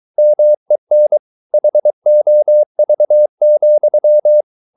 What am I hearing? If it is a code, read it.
Morse code, heard as a keyed tone, about 23 wpm, 600 Hz. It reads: MEN HOV,
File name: Ferie QRS 23wpm